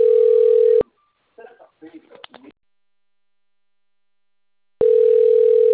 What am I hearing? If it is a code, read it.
Ringback tone (North America), 440 + 480 Hz, with speech.
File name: rg-1600-unknown- - - .49953